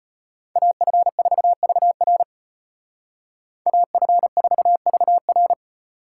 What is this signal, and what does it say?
Morse code, heard as a keyed tone, about 33 wpm, 700 Hz. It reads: AF4VR AF4VR